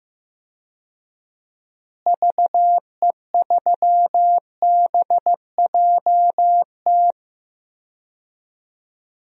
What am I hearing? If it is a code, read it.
Morse code, heard as a keyed tone, about 15 wpm, 700 Hz. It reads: VE3BJT
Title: DXCC-100 Call Signs - 6 Characters - Repeated 3x - 15wpm